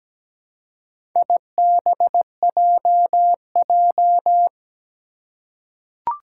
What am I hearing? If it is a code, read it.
Morse code, heard as a keyed tone, about 17 wpm, 700 Hz. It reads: IBJJ
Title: Four Challenging Letter-Number Combinations 17wpm